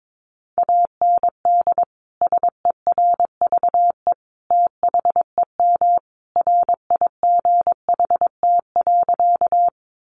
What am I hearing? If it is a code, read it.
Morse code, heard as a keyed tone, about 22 wpm, 700 Hz. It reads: AND SERVE THEM RIGHT.
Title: Sentences from Top 400 Words 22wpm